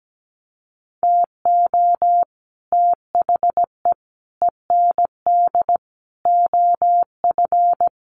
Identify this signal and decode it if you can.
Morse code, heard as a keyed tone, about 17 wpm, 700 Hz. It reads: TO THE END OF
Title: Sets of 4 Words 17wpm